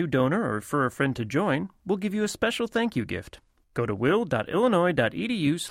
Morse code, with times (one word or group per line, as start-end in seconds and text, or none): none